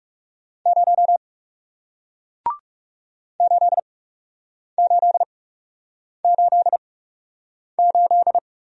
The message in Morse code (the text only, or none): 0 8888